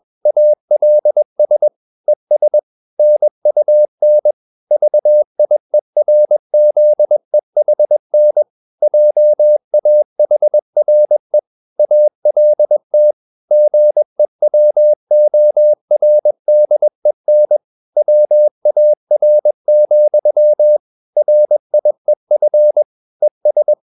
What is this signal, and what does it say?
Morse code, heard as a keyed tone, about 21 wpm, 600 Hz. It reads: ALS ES NUN VIERZEHN JAHRE ALT GEWORDEN WAR, RIEF ES
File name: marienkind2 21wpm